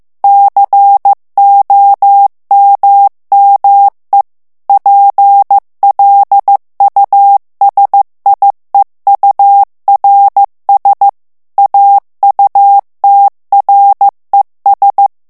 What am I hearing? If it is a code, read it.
Morse code, heard as a keyed tone, about 15 wpm, 800 Hz. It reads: COMME PLUSIEURS AUTRES
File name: morse mp3 / FRANCAIS TEXTE CLAIR 1 VITESSE 14 MOTS MINUTES